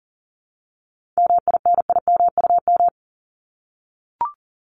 Morse code, heard as a keyed tone, about 39 wpm, 700 Hz. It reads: MINIMUM
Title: Top 1251-1500 Words 40wpm